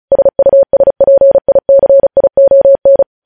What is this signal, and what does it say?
Morse code, heard as a keyed tone, about 36 wpm, 550 Hz. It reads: SUSPICION
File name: podcast / News headlines-1x-35x35